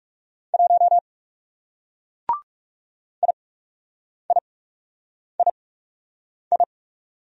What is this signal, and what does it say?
Morse code, heard as a keyed tone, about 40 wpm, spaced as for 8 wpm, 700 Hz. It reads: 1 IIII